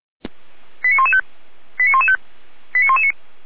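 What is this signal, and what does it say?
Five-tone selective call (ZVEI), heard as quick runs of five tones, tones of about 70 ms. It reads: 89115 89115 89119
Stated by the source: ZVEI five-tone selective call 89042 received on SDR.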